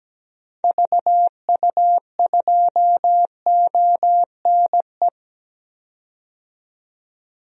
Morse code, heard as a keyed tone, about 17 wpm, 700 Hz. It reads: VU2ONE
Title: DXCC-100 Call Signs - 6 Characters - 17wpm